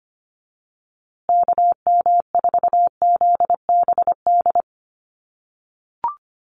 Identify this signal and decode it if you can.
Morse code, heard as a keyed tone, about 25 wpm, 700 Hz. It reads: KM4ZBD
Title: USA Call Signs Repeated 3x 25wpm